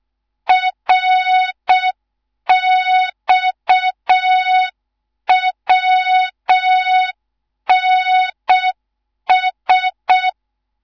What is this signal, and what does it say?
Morse code, heard as a keyed tone, about 6 wpm, 3700 Hz. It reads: RXWNS
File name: Morze-HA2MN / hang / 11 GYAKORLAT